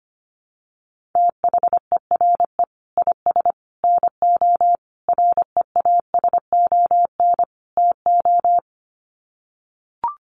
Morse code, heard as a keyed tone, about 25 wpm, 700 Hz. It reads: THERE IS NO REASON TO
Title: Sets of 5 Words Encore 25wpm